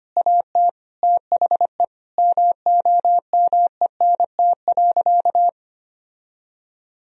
Morse code, heard as a keyed tone, about 25 wpm, 700 Hz. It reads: AT THE MOMENT.
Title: Sentences from Top 1000 Words - Review Part 3 - 25wpm